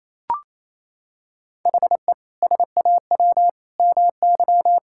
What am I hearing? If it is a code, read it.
Morse code, heard as a keyed tone, about 28 wpm, 700 Hz. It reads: HE SAW MY
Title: Sentences from Top 1600 Words 28wpm